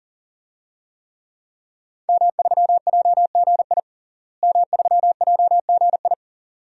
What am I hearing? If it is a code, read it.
Morse code, heard as a keyed tone, about 38 wpm, 700 Hz. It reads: M3JGI M3JGI
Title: DXCC-100 Call Signs - 5 Characters - Repeated 2x - 40wpm